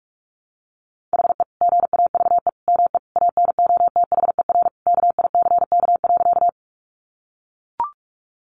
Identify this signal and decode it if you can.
Morse code, heard as a keyed tone, about 44 wpm, 700 Hz. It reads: HE GAVE ME ANOTHER KICK.
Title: Sentences from Top 1300 Words 45wpm